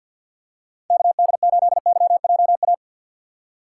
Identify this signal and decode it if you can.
Morse code, heard as a keyed tone, about 53 wpm, 700 Hz. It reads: KD8YJA
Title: USA Call Signs - 6x Character Spacing - 50wpm